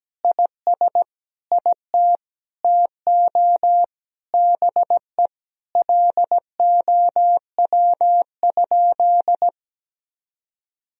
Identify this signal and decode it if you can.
Morse code, heard as a keyed tone, about 17 wpm, 700 Hz. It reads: IS IT TO BE LOW?
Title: Sentences from Top 1000 Words - Review Part 1 - 17wpm